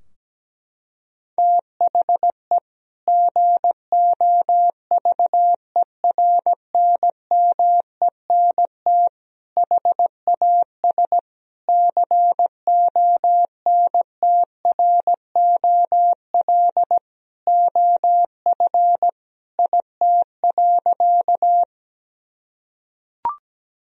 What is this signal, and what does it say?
Morse code, heard as a keyed tone, about 17 wpm, 700 Hz. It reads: THE GOVERNMENT HAS CONTROL OF IT.
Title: Sentences from Top 1000 Words - Review Part 3 - 17wpm